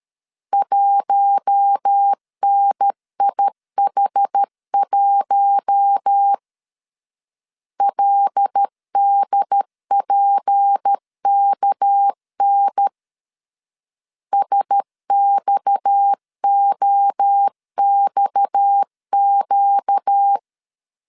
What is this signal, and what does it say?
Morse code, heard as a keyed tone, about 13 wpm, 800 Hz. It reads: 1NIH1 LDPKN SXOXQ
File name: LEZIONI / 6 PKX0 / 6 PKX 12wpm 2024